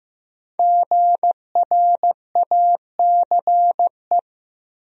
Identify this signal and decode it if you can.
Morse code, heard as a keyed tone, about 15 wpm, 700 Hz. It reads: GRACE
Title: USA Common Names - 15wpm